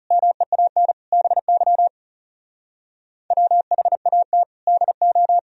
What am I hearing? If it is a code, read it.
Morse code, heard as a keyed tone, about 37 wpm, 700 Hz. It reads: MEAN BY WHAT DO
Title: Speed-Racing - Sets of 5 Words 30wpm